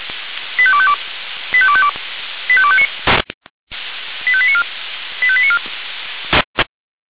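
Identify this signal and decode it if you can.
Five-tone selective call (ZVEI), heard as quick runs of five tones, tones of about 70 ms. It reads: 85252 85252 85259 85894 85894
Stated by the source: ZVEI five-tone selective call 85830, received by SDR.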